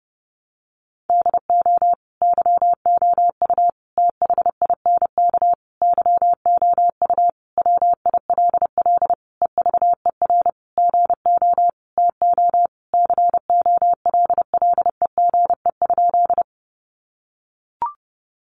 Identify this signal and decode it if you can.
Morse code, heard as a keyed tone, about 30 wpm, 700 Hz. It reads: DO YOU THINK YOU WILL EVER GO TO COLLEGE?
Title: Sentences from Top 1300 Words 30wpm